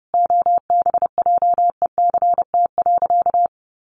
Morse code, heard as a keyed tone, about 30 wpm, 700 Hz. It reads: OBJECT.
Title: Sentences from Top 1400 Words 30wpm